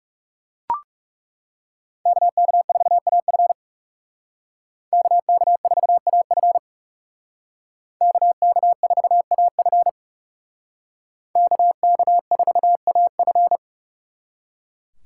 Morse code, heard as a keyed tone, about 37 wpm, 700 Hz. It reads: KK4AF KK4AF KK4AF KK4AF
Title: Speed Racing - USA 2x2 Call Signs 30wpm